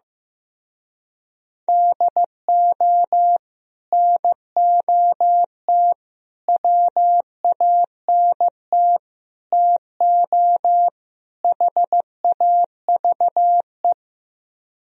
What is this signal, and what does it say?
Morse code, heard as a keyed tone, about 15 wpm, 700 Hz. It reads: DO NOT WANT TO HAVE
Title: Speed-Racing - Sets of 5 Words 15wpm